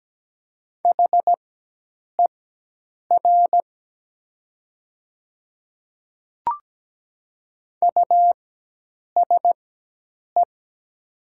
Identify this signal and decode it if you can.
Morse code, heard as a keyed tone, about 17 wpm, spaced as for 8 wpm, 700 Hz. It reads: HER USE
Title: 4x Character Spacing - Top 100 Words - 17wpm